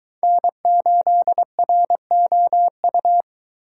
Morse code, heard as a keyed tone, about 21 wpm, 700 Hz. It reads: N8ROU